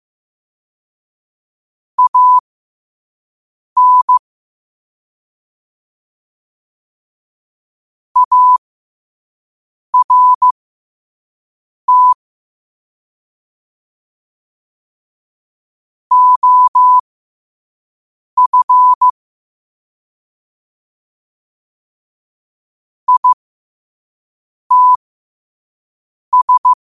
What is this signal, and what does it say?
Morse code, heard as a keyed tone, about 15 wpm, spaced as for 5 wpm, 1000 Hz. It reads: AN ART OF ITS
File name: qotdfiles / 5wpm